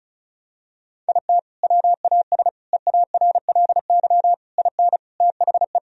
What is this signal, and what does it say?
Morse code, heard as a keyed tone, about 36 wpm, 700 Hz. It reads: IT WAS EARLY IN THE
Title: Sentences from Top 1000 Words - Review Part 2 - 35wpm